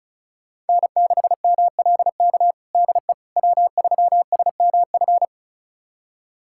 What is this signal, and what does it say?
Morse code, heard as a keyed tone, about 36 wpm, 700 Hz. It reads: N6MLK DE W3SMF
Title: USA Callsign DE Callsign - 5 Characters - 35wpm